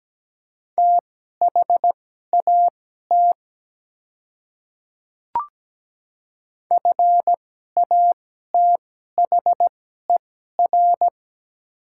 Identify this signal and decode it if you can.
Morse code, heard as a keyed tone, about 17 wpm, spaced as for 12 wpm, 700 Hz. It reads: THAT FATHER
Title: 2x Character Spacing - Top 300 Words - 17wpm